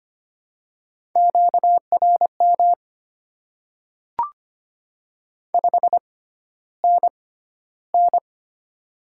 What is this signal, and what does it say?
Morse code, heard as a keyed tone, about 25 wpm, spaced as for 13 wpm, 700 Hz. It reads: QRM 5NN
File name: Top 90 QSO Elements - 6x Character Spacing - 25wpm